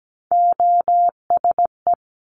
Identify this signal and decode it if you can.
Morse code, heard as a keyed tone, about 17 wpm, 700 Hz. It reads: OSE
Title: Top Three Letter Combinations in Words 17wpm